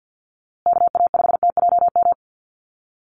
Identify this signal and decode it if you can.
Morse code, heard as a keyed tone, about 50 wpm, 700 Hz. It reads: KA5TJM
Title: USA Call Signs Repeated 3x 50wpm